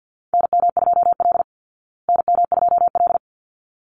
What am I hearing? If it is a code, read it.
Morse code, heard as a keyed tone, about 50 wpm, 700 Hz. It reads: NM2L NM2L
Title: Call Signs Repeated 3x - 50wpm